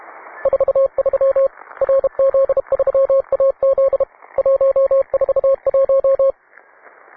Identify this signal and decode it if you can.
Morse code, heard as a keyed tone, about 31 wpm, 540 Hz. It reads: 43 RZ3AZ 141